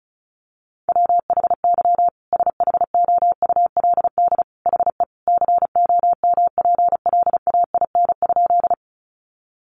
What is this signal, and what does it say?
Morse code, heard as a keyed tone, about 34 wpm, 700 Hz. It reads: WHY SHOULD HE COMPLAIN?
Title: Sentences from Top 1300 Words 35wpm